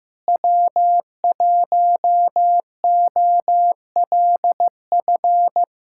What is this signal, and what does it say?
Morse code, heard as a keyed tone, about 15 wpm, 700 Hz. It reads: W1OLF
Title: USA Callsign DE Callsign - 5 Characters - 15wpm